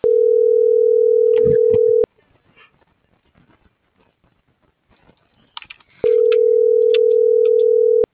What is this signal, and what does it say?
Ringback tone (North America), 440 + 480 Hz.